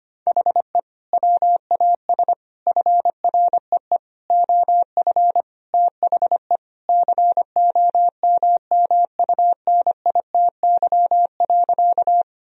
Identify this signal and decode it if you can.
Morse code, heard as a keyed tone, about 25 wpm, 700 Hz. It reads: HE WAS FREE OF THE COMMUNITY.
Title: Sentences from Top 1000 Words - Review Part 4 - 25wpm